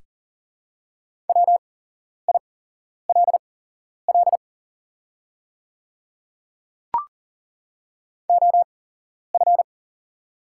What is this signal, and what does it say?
Morse code, heard as a keyed tone, about 40 wpm, spaced as for 11 wpm, 690 Hz. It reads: WILL OF
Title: 8x Character Spacing - Top 200 Words - 40wpm